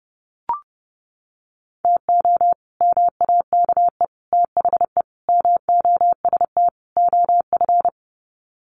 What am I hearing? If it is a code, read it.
Morse code, heard as a keyed tone, about 30 wpm, 700 Hz. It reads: TO MAKE THE MOST OF
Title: Sets of 5 Words Encore2 30wpm